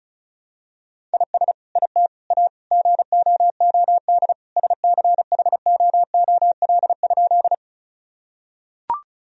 Morse code, heard as a keyed tone, about 36 wpm, 700 Hz. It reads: IS IT A GOOD SCHOOL?
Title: Sentences from Top 1000 Words - Review Part 1 - 35wpm